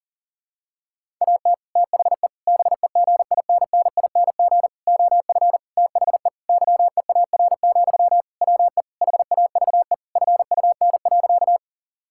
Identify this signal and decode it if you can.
Morse code, heard as a keyed tone, about 40 wpm, 700 Hz. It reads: AT THE BEGINNING OF THE YEAR, WE HAVE FUN.
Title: Sentences from Top 1000 Words - Review Part 5 - 40wpm